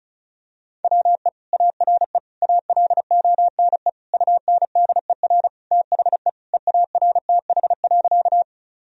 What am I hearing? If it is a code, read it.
Morse code, heard as a keyed tone, about 36 wpm, 700 Hz. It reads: WE ARE ALONE UNDER THE EARTH.
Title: Sentences from Top 1000 Words - Review Part 1 - 35wpm